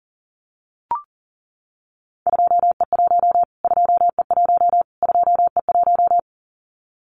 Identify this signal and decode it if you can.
Morse code, heard as a keyed tone, about 40 wpm, 700 Hz. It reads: 2E1 2E1 2E1
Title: Callsign Prefixes for Top 25 DXCC - Repeated 3x - 40wpm